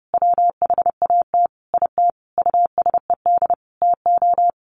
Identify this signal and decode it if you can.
Morse code, heard as a keyed tone, about 30 wpm, 700 Hz. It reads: WHAT IT USED TO
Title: Sentences from Top 1100 Words 30wpm